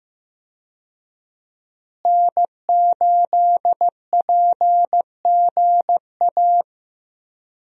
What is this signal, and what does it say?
Morse code, heard as a keyed tone, about 15 wpm, 700 Hz. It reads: N8PGA
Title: USA Call Signs - 5 Characters - Repeated 2x - 15wpm